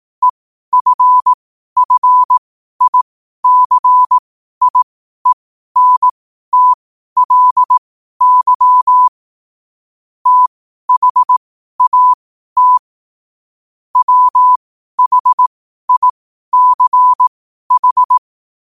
Morse code, heard as a keyed tone, about 18 wpm, spaced as for 12 wpm, 1000 Hz. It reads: EFFICIENTLY THAT WHICH